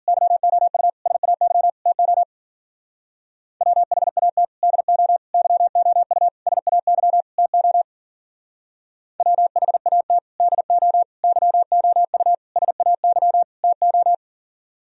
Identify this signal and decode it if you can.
Morse code, heard as a keyed tone, about 52 wpm, 700 Hz. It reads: YOU SAY TO WHAT DO YOU SAY TO WHAT DO YOU SAY TO